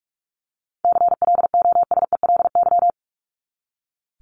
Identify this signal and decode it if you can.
Morse code, heard as a keyed tone, about 44 wpm, 700 Hz. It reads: CLOSELY
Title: Top 1251-1500 Words 45wpm